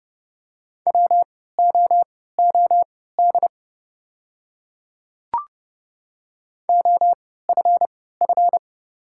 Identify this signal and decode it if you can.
Morse code, heard as a keyed tone, about 31 wpm, spaced as for 17 wpm, 700 Hz. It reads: WOOD OFF
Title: 3x Character Spacing - Top 300 Words - 30wpm